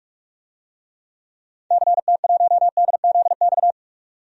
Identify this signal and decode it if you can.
Morse code, heard as a keyed tone, about 47 wpm, 700 Hz. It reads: CT1DZX
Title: DXCC-100 Call Signs - 6 Characters - 45wpm